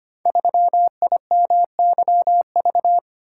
Morse code, heard as a keyed tone, about 25 wpm, 700 Hz. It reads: 3IMYV
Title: Code Groups - 5 Letter:Numbers - 25wpm